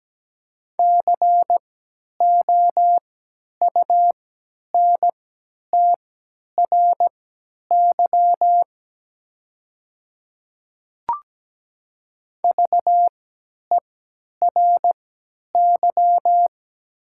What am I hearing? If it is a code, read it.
Morse code, heard as a keyed tone, about 17 wpm, spaced as for 10 wpm, 700 Hz. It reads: COUNTRY VERY
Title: 3x Character Spacing - Top 300 Words - 17wpm